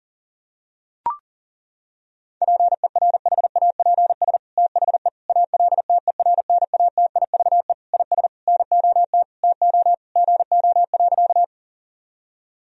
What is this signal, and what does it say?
Morse code, heard as a keyed tone, about 37 wpm, 700 Hz. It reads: PERHAPS THE ALTERNATIVE IS NOT TO GO.
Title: Sentences from Top 1000 Words - Review Part 4 - 40wpm